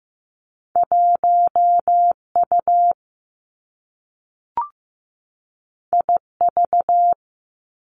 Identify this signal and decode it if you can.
Morse code, heard as a keyed tone, about 15 wpm, 700 Hz. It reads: 1 U I V
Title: Two Challenging Letter-Number Combinations 15wpm